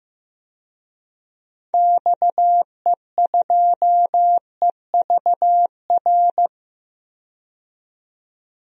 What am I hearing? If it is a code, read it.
Morse code, heard as a keyed tone, about 15 wpm, 700 Hz. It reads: XE2EVR
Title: DXCC-100 Call Signs - 6 Characters - 15wpm